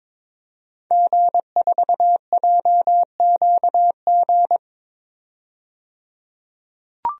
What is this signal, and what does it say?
Morse code, heard as a keyed tone, about 22 wpm, 700 Hz. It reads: G4JQG
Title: DXCC-100 Call Signs - 5 Characters - 22wpm